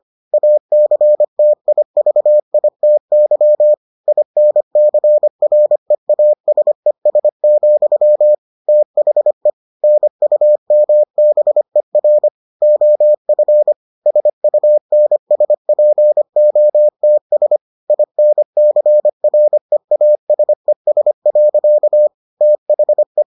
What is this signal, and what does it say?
Morse code, heard as a keyed tone, about 25 wpm, 600 Hz. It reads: ACTIVITY INCREASES, THE NUMBER OF SUNSPOTS INCREASES. THE